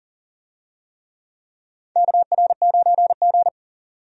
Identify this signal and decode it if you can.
Morse code, heard as a keyed tone, about 40 wpm, 700 Hz. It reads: KR9G
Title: USA Call Signs - 4 Characters - 40wpm